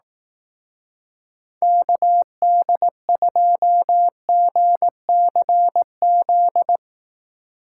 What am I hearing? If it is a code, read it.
Morse code, heard as a keyed tone, about 18 wpm, 700 Hz. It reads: KD2GCZ
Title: Speed Racing - USA 2x3 Call Signs 15wpm